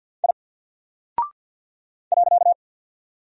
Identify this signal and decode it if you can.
Morse code, heard as a keyed tone, about 50 wpm, 700 Hz. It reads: I .
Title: Single Character (Letter:Number:Punctuation:Prosign) Alt 50wpm